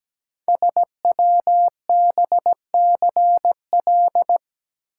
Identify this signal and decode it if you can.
Morse code, heard as a keyed tone, about 17 wpm, 700 Hz. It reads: SWBCL
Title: Code Groups - 5 Letters - 17wpm